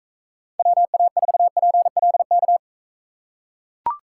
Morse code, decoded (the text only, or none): WA4PLK